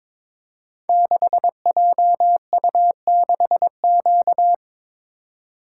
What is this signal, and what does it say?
Morse code, heard as a keyed tone, about 23 wpm, 700 Hz. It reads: 6JU6Q
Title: Code Groups - 5 Letter:Numbers - 22wpm